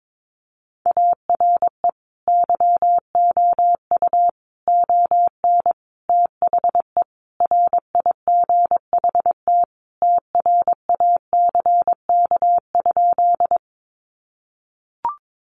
Morse code, heard as a keyed tone, about 22 wpm, 700 Hz. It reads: ARE YOU ON THE RIGHT TRACK?
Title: Sentences from Top 1100 Words 22wpm